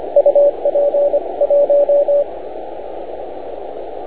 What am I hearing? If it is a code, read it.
Morse code, heard as a keyed tone, about 25 wpm, 570 Hz, with radio noise. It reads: UP 1